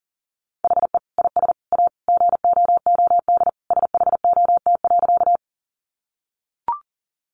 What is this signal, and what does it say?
Morse code, heard as a keyed tone, about 37 wpm, 700 Hz. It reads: HE IS A GOOD SHOT.
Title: Sentences from Top 1100 Words 40wpm